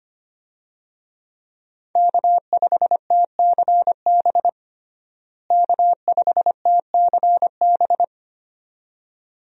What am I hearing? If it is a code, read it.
Morse code, heard as a keyed tone, about 25 wpm, 700 Hz. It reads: K5TCB K5TCB
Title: USA Call Signs - 5 Characters - Repeated 2x - 25wpm